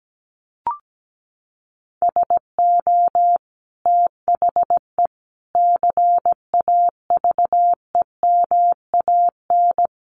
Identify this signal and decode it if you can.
Morse code, heard as a keyed tone, about 17 wpm, 700 Hz. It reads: SO THE CAVEMAN
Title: A Sound Of Thunder 17wpm